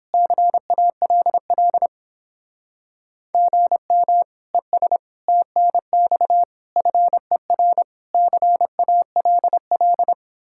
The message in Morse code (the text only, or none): CALL GM ES TNX FER CALL